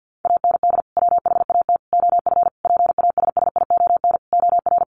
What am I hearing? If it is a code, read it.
Morse code, heard as a keyed tone, about 50 wpm, 700 Hz. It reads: AND WHAT OF PASSION OR
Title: Sentences from Top 1400 Words 50wpm